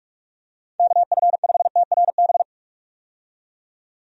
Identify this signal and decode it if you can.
Morse code, heard as a keyed tone, about 46 wpm, 700 Hz. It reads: KF5TRB
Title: USA Call Signs - 6 Characters - 45wpm